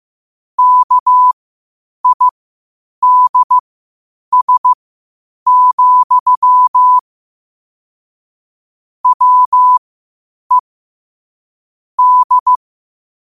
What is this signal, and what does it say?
Morse code, heard as a keyed tone, about 15 wpm, spaced as for 8 wpm, 1000 Hz. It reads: KIDS, WE D
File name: qotdfiles / 8wpm